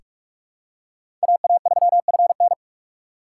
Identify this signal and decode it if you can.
Morse code, heard as a keyed tone, about 45 wpm, 700 Hz. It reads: AA3FN
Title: USA Call Signs - 5 Characters - 4x Character Spacing - 45wpm